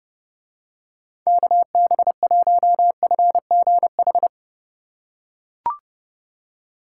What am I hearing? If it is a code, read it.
Morse code, heard as a keyed tone, about 29 wpm, 700 Hz. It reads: KB1FGH